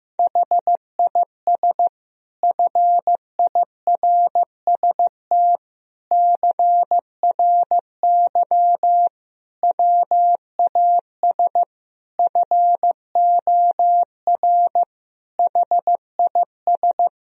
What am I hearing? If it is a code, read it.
Morse code, heard as a keyed tone, about 15 wpm, 700 Hz. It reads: HIS FIRST CRY WAS FOR HIS
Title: Sentences from Top 1000 Words - Review Part 2 - 15wpm